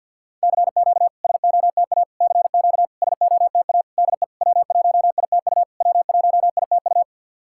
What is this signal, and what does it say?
Morse code, heard as a keyed tone, about 53 wpm, 690 Hz. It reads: CQ SOTA CQ SOTA DE W1ITU W1ITU
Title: SOTA QSO Practice 50wpm